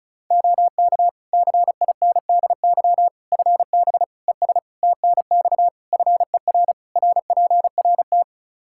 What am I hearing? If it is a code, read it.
Morse code, heard as a keyed tone, about 34 wpm, 700 Hz. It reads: OK CINDY FB ES TNX FER RPRT
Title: K1USN SST QSO Practice 35wpm